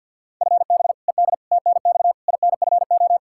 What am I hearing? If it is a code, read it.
Morse code, heard as a keyed tone, about 47 wpm, 700 Hz. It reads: FB ED TNX INFO
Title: K1USN SST QSO Practice 50wpm